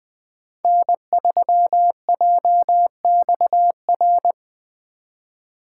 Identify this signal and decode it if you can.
Morse code, heard as a keyed tone, about 20 wpm, 700 Hz. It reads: N3JXR
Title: USA Call Signs - 5 Characters - 4x Character Spacing - 20wpm